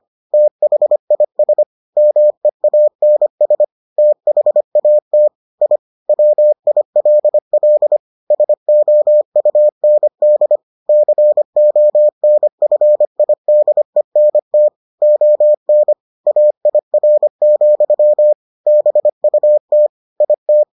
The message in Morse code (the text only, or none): THIS MEANS THAT I WILL SOUND CONFIDENT ON AIR, BUT IT